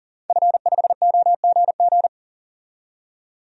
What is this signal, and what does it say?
Morse code, heard as a keyed tone, about 40 wpm, 700 Hz. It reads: F5OGG
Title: DXCC-100 Call Signs - 5 Characters - 40wpm